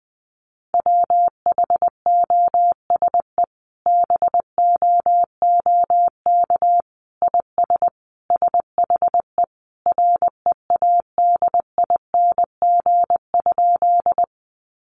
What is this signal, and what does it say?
Morse code, heard as a keyed tone, about 19 wpm, 700 Hz. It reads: WHOSE BOOK IS SHE READING?
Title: Sentences from Top 1200 Words 20wpm